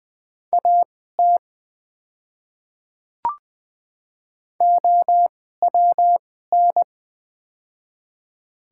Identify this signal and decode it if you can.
Morse code, heard as a keyed tone, about 20 wpm, spaced as for 14 wpm, 700 Hz. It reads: AT OWN
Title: 2x Character Spacing - Top 300 Words - 20wpm